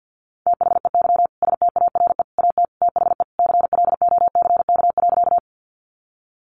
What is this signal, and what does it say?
Morse code, heard as a keyed tone, about 50 wpm, 700 Hz. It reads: THEY STARE AT THE CLOCK.
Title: Easy Phrases 50wpm